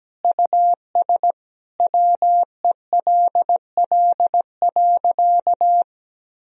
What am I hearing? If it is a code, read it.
Morse code, heard as a keyed tone, about 17 wpm, 700 Hz. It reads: US WELL.
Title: Sentences from Top 1000 Words - Review Part 4 - 17wpm